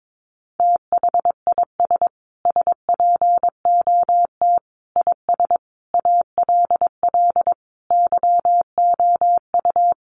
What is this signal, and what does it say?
Morse code, heard as a keyed tone, about 22 wpm, 700 Hz. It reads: THIS SPOT IS ALL YOU
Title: Sentences from Top 1100 Words 22wpm